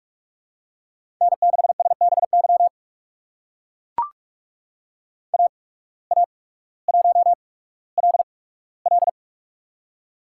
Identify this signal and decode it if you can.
Morse code, heard as a keyed tone, about 44 wpm, spaced as for 19 wpm, 700 Hz. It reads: N6SBY AA1LL
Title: USA Call Signs - 5 Characters - 8x Character Spacing - 45wpm